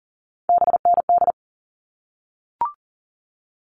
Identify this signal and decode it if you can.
Morse code, heard as a keyed tone, about 40 wpm, 700 Hz. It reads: BND
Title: All Cut Numbers - 3-Digit 40wpm